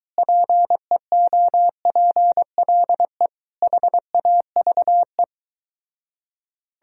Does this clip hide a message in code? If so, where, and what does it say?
Yes: Morse code from 0.2 to 5.2 s, heard as a keyed tone, about 24 wpm, 700 Hz. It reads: PEOPLE HAVE